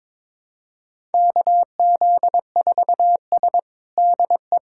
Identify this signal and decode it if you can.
Morse code, heard as a keyed tone, about 21 wpm, 700 Hz. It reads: KZ4S DE
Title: USA Callsign DE Callsign - 4 Characters - 22wpm